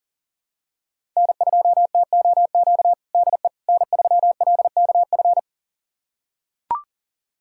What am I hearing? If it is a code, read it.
Morse code, heard as a keyed tone, about 40 wpm, 700 Hz. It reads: N2TOQ DE N3LKF
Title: USA Callsign DE Callsign - 5 Characters - 40wpm